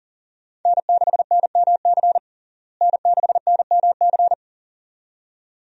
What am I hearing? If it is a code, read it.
Morse code, heard as a keyed tone, about 40 wpm, 700 Hz. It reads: N6NMC N6NMC
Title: USA Call Signs - 5 Characters - Repeated 2x - 40wpm